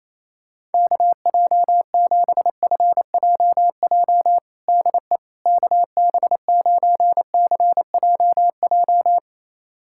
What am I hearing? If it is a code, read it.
Morse code, heard as a keyed tone, about 28 wpm, 700 Hz. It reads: KJ7FJJ DE KB9CJJ
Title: USA Callsign DE Callsign - 6 Characters - 28wpm